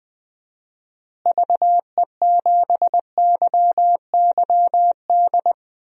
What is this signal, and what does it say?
Morse code, heard as a keyed tone, about 20 wpm, 700 Hz. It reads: VE7YYD
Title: DXCC-100 Call Signs - 6 Characters - Repeated 2x - 20wpm